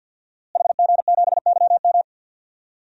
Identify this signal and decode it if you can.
Morse code, heard as a keyed tone, about 51 wpm, 700 Hz. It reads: HG7YM
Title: DXCC-100 Call Signs - 5 Characters - Repeated 2x - 50wpm